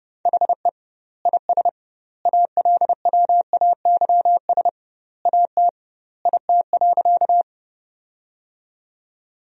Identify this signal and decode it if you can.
Morse code, heard as a keyed tone, about 30 wpm, 700 Hz. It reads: HE IS ALWAYS AT IT.
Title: 2x Word-Spacing - Sentences from Top 300 Words - 30wpm